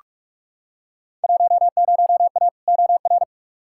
Morse code, heard as a keyed tone, about 46 wpm, 700 Hz. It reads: 10A OR
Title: Field Day Exchanges - 45wpm